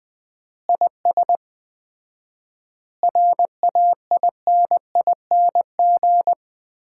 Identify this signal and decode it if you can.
Morse code, heard as a keyed tone, about 20 wpm, 700 Hz. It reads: IS RAINING
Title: Common QSO Phrases - 4x Word-Spacing - 20wpm